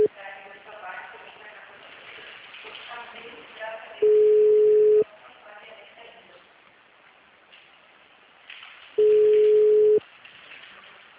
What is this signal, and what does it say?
Ringback tone (Europe, 425 Hz), with speech.
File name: out- -2013- - - .12